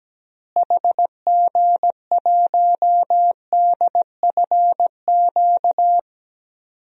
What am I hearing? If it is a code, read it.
Morse code, heard as a keyed tone, about 17 wpm, 700 Hz. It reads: HG1DFQ
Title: DXCC-100 Call Signs - 6 Characters - Repeated 3x - 17wpm